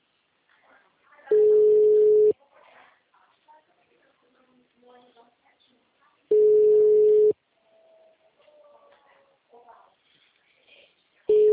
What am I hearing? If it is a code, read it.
Ringback tone (Europe, 425 Hz), with speech.
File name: out- -2019- - - .48